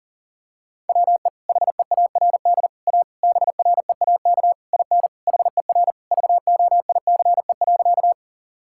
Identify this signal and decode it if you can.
Morse code, heard as a keyed tone, about 40 wpm, 700 Hz. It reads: WE HEARD A BREAK IN HER VOICE.
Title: Sentences from Top 1000 Words - Review Part 6 - 40wpm